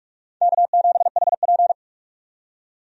Morse code, heard as a keyed tone, about 46 wpm, 700 Hz. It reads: K7HP
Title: USA Call Signs - 4 Characters - 45wpm